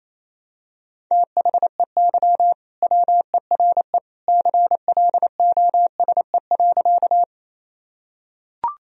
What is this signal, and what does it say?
Morse code, heard as a keyed tone, about 28 wpm, 700 Hz. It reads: THEY WERE CLOSE.
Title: Sentences from Top 1000 Words - Review Part 1 - 28wpm